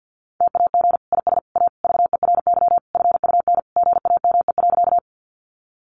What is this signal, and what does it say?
Morse code, heard as a keyed tone, about 50 wpm, 700 Hz. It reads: TAG IS A VERY FUN GAME.
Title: Additional Easy Phrases 50wpm